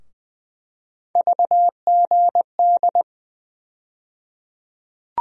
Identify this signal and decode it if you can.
Morse code, heard as a keyed tone, about 20 wpm, 700 Hz. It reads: VGD
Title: USA Call Sign Postfixes - 3 Letters 20wpm